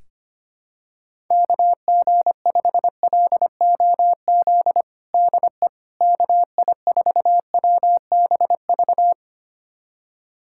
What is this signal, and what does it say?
Morse code, heard as a keyed tone, about 25 wpm, 700 Hz. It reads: KG5LOZ DE KI4WBV
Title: USA Callsign DE Callsign - 6 Characters - 25wpm